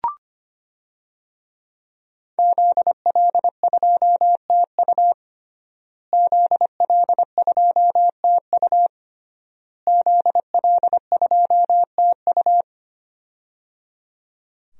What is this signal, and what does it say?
Morse code, heard as a keyed tone, about 25 wpm, 700 Hz. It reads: ZL2TU ZL2TU ZL2TU
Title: DXCC-100 Call Signs - 5 Characters - Repeated 3x - 25wpm